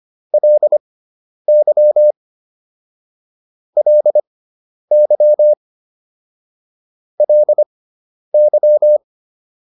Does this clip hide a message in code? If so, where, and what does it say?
Yes: Morse code from 0.3 to 9.0 s, heard as a keyed tone, about 25 wpm, spaced as for 10 wpm, 600 Hz. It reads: LY LY LY